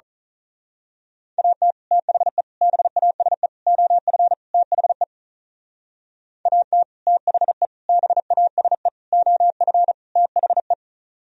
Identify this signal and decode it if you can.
Morse code, heard as a keyed tone, about 38 wpm, 700 Hz. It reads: AT THE BASE OF THE AT THE BASE OF THE
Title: Speed-Racing - Sets of 5 Words 35wpm